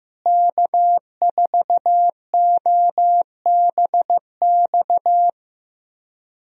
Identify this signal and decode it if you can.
Morse code, heard as a keyed tone, about 15 wpm, 700 Hz. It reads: K4OBX
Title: USA Call Signs - 5 Characters - Repeated 2x - 15wpm